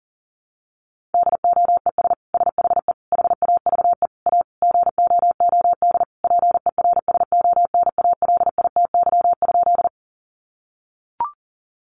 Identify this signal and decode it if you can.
Morse code, heard as a keyed tone, about 40 wpm, 700 Hz. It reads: DOES SHE HAVE A GOOD PERSONALITY?
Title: Sentences from Top 1400 Words 40wpm